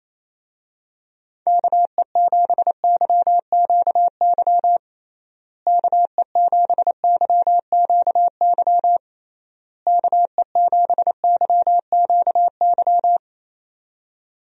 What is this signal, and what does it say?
Morse code, heard as a keyed tone, about 28 wpm, 700 Hz. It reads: KE7YQY KE7YQY KE7YQY
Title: USA Call Signs - 6 Characters - Repeated 3x - 28wpm